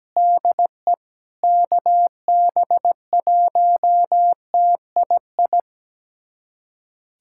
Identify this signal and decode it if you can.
Morse code, heard as a keyed tone, about 17 wpm, 700 Hz. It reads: DE KB1TII
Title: USA Callsign DE Callsign 17wpm